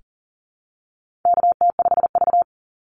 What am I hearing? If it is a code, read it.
Morse code, heard as a keyed tone, about 40 wpm, 700 Hz. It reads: KT5V
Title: Call Signs Repeated 2x - 40wpm